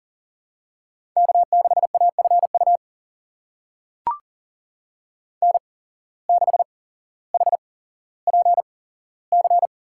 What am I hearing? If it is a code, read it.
Morse code, heard as a keyed tone, about 40 wpm, spaced as for 17 wpm, 700 Hz. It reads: K6AFU N6HPC
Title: USA Call Signs - 8x Character Spacing - 40wpm